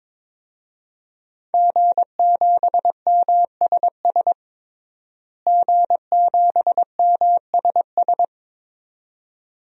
Morse code, heard as a keyed tone, about 22 wpm, 700 Hz. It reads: G7MSS G7MSS
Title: DXCC-100 Call Signs - 5 Characters - Repeated 2x - 22wpm